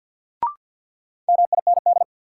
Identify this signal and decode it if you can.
Morse code, heard as a keyed tone, about 53 wpm, 700 Hz. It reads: MIND
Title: Sets of 5 Words - New Word Drill - 50wpm